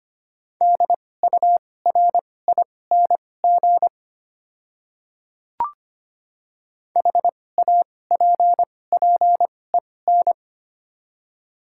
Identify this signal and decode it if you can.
Morse code, heard as a keyed tone, about 25 wpm, spaced as for 18 wpm, 700 Hz. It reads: DURING HAPPEN
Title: 2x Character Spacing - Top 500 Words - 25wpm